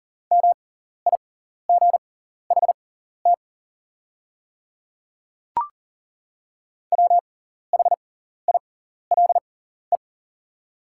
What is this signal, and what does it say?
Morse code, heard as a keyed tone, about 40 wpm, spaced as for 14 wpm, 700 Hz. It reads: MIGHT WHILE
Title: 6x Character Spacing - Top 300 Words - 40wpm